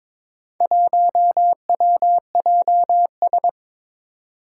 Morse code, heard as a keyed tone, about 21 wpm, 700 Hz. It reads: 1WJS
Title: Four Challenging Letter-Number Combinations 22wpm